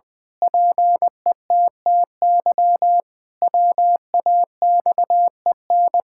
Morse code, heard as a keyed tone, about 21 wpm, 700 Hz. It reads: PETTY WAXEN
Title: Pangram Challenge v2 - 20wpm